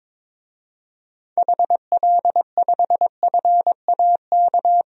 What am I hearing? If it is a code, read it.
Morse code, heard as a keyed tone, about 22 wpm, 700 Hz. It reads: HL5FAK
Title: DXCC-100 Call Signs - 6 Characters - 22wpm